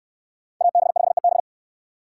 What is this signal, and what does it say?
Morse code, heard as a keyed tone, about 63 wpm, 700 Hz. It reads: AB5EB